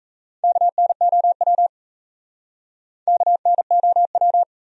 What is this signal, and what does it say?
Morse code, heard as a keyed tone, about 34 wpm, 700 Hz. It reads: KNOW KNOW
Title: Speed-Racing Top 100 Words 28wpm